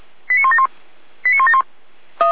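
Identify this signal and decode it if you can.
Five-tone selective call (ZVEI), heard as quick runs of five tones, tones of about 70 ms. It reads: 89171 89171
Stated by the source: ZVEI five-tone selective call received by SDR, recorded on the Feuerwehr Orlamünde audio server.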